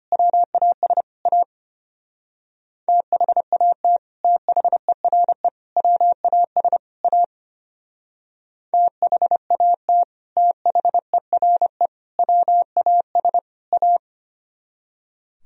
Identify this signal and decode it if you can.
Morse code, heard as a keyed tone, about 31 wpm, 700 Hz. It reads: WAS A THAT THERE WAS A THAT THERE WAS A
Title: Speed-Racing - Sets of 4 Words 25wpm